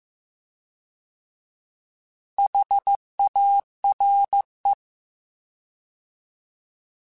Morse code, heard as a keyed tone, about 15 wpm, 800 Hz. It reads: HARE
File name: Alphabetical / Alphabetical mp3s / Hs @13